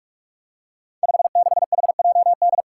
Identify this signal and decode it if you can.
Morse code, heard as a keyed tone, about 46 wpm, 700 Hz. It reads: 56HJD
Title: Five Challenging Letter-Number Combinations 45wpm